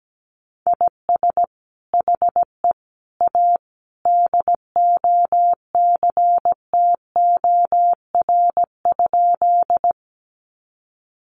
Morse code, heard as a keyed tone, about 17 wpm, 700 Hz. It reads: IS HE A DOCTOR?
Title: Sentences from Top 1500 Words 17wpm